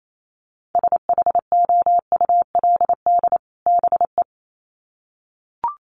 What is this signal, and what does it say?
Morse code, heard as a keyed tone, about 28 wpm, 700 Hz. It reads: SHOULD BE
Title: Sets of 2 Words 28wpm